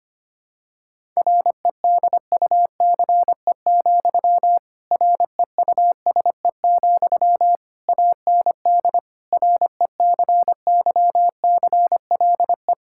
Morse code, heard as a keyed tone, about 25 wpm, 700 Hz. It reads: REDUCE, REUSE, AND RECYCLE
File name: practice / 25wpm / Difficult Sentences with Hints - 25wpm